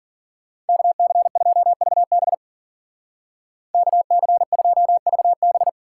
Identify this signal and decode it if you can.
Morse code, heard as a keyed tone, about 50 wpm, 700 Hz. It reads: KC2VB KC2VB